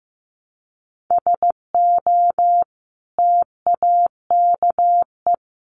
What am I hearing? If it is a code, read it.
Morse code, heard as a keyed tone, about 15 wpm, 700 Hz. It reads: SO TAKE
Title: Sentences from Top 300 Words 15wpm